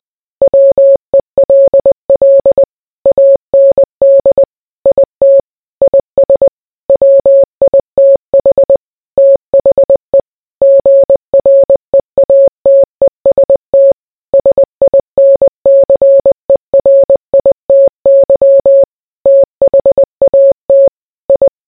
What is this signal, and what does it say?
Morse code, heard as a keyed tone, about 20 wpm, 550 Hz. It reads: WELL AND IT IS WITH THE GREATEST SINCERITY THAT I